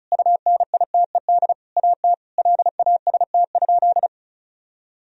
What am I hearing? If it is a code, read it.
Morse code, heard as a keyed tone, about 36 wpm, 700 Hz. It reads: UNITED AT LAST?
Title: Sentences from Top 1000 Words - Review Part 4 - 35wpm